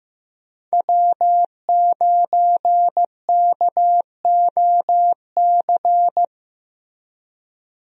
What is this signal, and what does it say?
Morse code, heard as a keyed tone, about 15 wpm, 700 Hz. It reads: W9KOC
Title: USA Call Signs - 5 Characters - Repeated 3x - 15wpm